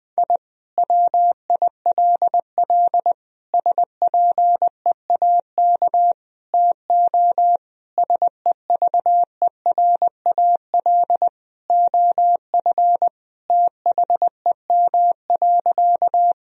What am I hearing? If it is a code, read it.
Morse code, heard as a keyed tone, about 20 wpm, 700 Hz. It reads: I WILL SPEAK TO SEVERAL OF THEM.
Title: Sentences from Top 1000 Words - Review Part 2 - 20wpm